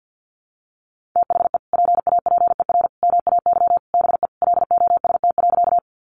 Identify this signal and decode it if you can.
Morse code, heard as a keyed tone, about 50 wpm, 700 Hz. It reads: THE PAPER MAY BE LOST.
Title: Sentences from Top 1400 Words 50wpm